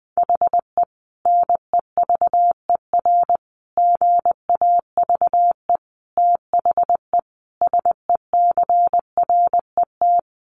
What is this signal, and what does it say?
Morse code, heard as a keyed tone, about 20 wpm, 700 Hz. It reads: HE NEVER GAVE THE SECRET
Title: Sentences from Top 1200 Words 20wpm